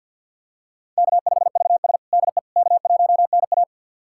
Morse code, heard as a keyed tone, about 52 wpm, 700 Hz. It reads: K5VS DE K1NA
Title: USA Callsign DE Callsign - 4 Characters - 50wpm